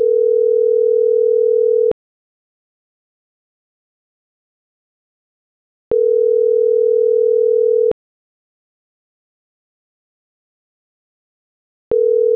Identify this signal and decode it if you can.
Ringback tone (North America), 440 + 480 Hz.
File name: CENTRAL MAERSK / out adolfo.msk - .26572